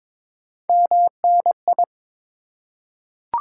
Morse code, heard as a keyed tone, about 22 wpm, 690 Hz. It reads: MNI
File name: Top 90 QSO Elements - 3x Character Spacing - 22wpm